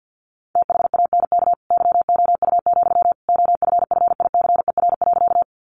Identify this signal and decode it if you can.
Morse code, heard as a keyed tone, about 50 wpm, 700 Hz. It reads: THANK YOU, OFFICER.
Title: Sentences from Top 1200 Words 50wpm